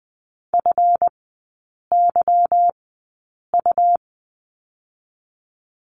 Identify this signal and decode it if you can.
Morse code, heard as a keyed tone, about 20 wpm, spaced as for 8 wpm, 700 Hz. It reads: FYU
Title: 172 Lesson 28e - Number 9 - Three Characters 20wpm